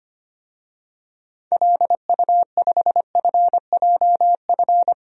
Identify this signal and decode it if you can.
Morse code, heard as a keyed tone, about 25 wpm, 700 Hz. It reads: LU5FJF
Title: DXCC-100 Call Signs - 6 Characters - 25wpm